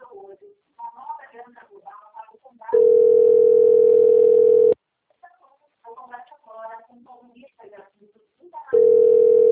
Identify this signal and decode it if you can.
Ringback tone (North America), 440 + 480 Hz, with speech.
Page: Jaguariuna - SP